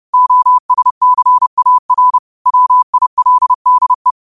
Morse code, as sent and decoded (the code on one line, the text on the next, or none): --- ... -.-. .- .-. / .-- .. .-.. -.. .
OSCAR WILDE